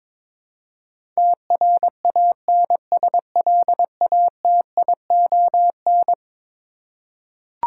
Morse code, heard as a keyed tone, about 22 wpm, 700 Hz. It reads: TRANSLATION
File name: practice / 22wpm / Eleven Letter Words - 22wpm